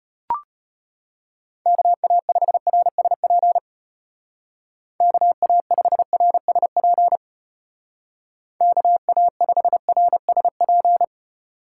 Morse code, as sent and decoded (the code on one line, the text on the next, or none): -.- .- ..... .-. ... .--. / -.- .- ..... .-. ... .--. / -.- .- ..... .-. ... .--.
KA5RSP KA5RSP KA5RSP